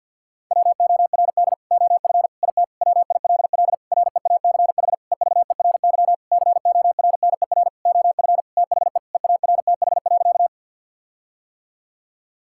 Morse code, heard as a keyed tone, about 50 wpm, 700 Hz. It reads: WORD OF IT WILL REACH EVERY CORNER OF THE EARTH.
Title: Sentences from Top 1600 Words 50wpm